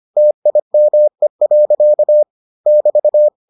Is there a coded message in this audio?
Yes: Morse code from 0.2 to 3.3 s, heard as a keyed tone, about 25 wpm, 600 Hz. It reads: TIME. <BT>